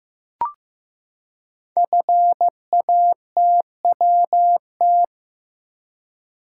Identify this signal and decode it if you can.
Morse code, heard as a keyed tone, about 15 wpm, 700 Hz. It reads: FATWT
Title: Code Groups - 5 Letters - 15wpm